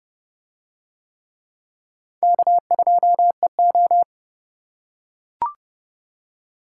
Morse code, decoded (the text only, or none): K2EO